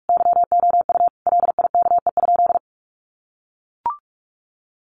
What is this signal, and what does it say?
Morse code, heard as a keyed tone, about 44 wpm, 700 Hz. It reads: YOU LIKE?
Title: Sentences from Top 1200 Words 45wpm